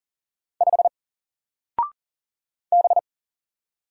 Morse code, heard as a keyed tone, about 38 wpm, 700 Hz. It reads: 5 B
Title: Single Character (Letter:Number:Punctuation:Prosign) 40wpm